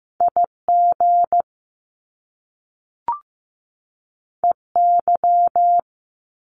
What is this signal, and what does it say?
Morse code, heard as a keyed tone, about 15 wpm, 700 Hz. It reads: IG EY
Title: Top Two Letter Combinations in Words 15wpm